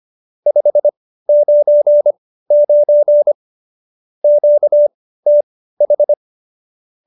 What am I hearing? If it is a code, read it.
Morse code, heard as a keyed tone, about 25 wpm, spaced as for 15 wpm, 590 Hz. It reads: 599 QTH